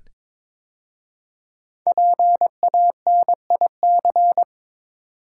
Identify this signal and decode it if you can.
Morse code, heard as a keyed tone, about 22 wpm, 700 Hz. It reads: PANIC
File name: 5 Letter Words - First 3 Letters Given - 22wpm